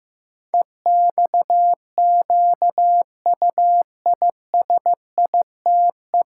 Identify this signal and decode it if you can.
Morse code, heard as a keyed tone, about 15 wpm, 700 Hz. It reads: EXQUISITE
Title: Pangram Challenge v2 - 15wpm